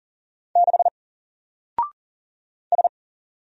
Morse code, heard as a keyed tone, about 40 wpm, 700 Hz. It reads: 6 S
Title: Single Letter-Number 40wpm